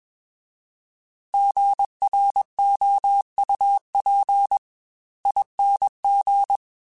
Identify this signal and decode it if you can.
Morse code, heard as a keyed tone, about 21 wpm, spaced as for 18 wpm, 800 Hz. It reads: GROUPING
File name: Alphabetical / Alphabetical mp3s / Gs @20